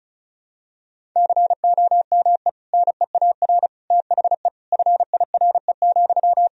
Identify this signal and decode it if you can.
Morse code, heard as a keyed tone, about 36 wpm, 700 Hz. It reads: COME NEAR THE FIRE,
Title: Sentences from Top 1000 Words - Review Part 1 - 35wpm